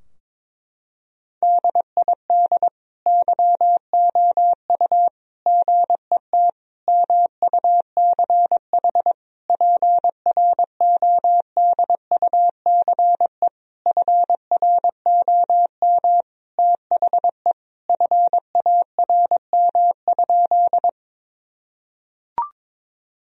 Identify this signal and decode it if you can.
Morse code, heard as a keyed tone, about 22 wpm, 690 Hz. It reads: DID YOU GET MUCH PRODUCE FROM THE FARM?
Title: Sentences from Top 1000 Words - Review Part 3 - 22wpm